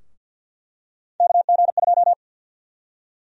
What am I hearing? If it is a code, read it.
Morse code, heard as a keyed tone, about 48 wpm, 700 Hz. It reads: KG2
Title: USA Call Sign Prefixes - Repeated 2x - 50wpm